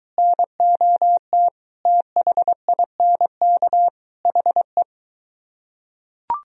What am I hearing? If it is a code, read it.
Morse code, heard as a keyed tone, about 20 wpm, 700 Hz. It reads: NOT THINK HE